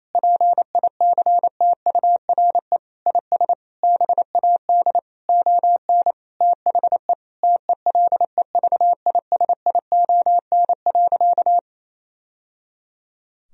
Morse code, heard as a keyed tone, about 28 wpm, 700 Hz. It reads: PICTURE IS BAD ON THE TELEVISION.
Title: Sentences from Top 1000 Words - Review Part 4 - 28wpm